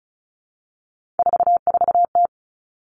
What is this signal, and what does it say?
Morse code, heard as a keyed tone, about 36 wpm, 700 Hz. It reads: 44T
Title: Three Challenging Letter-Number Combinations 35wpm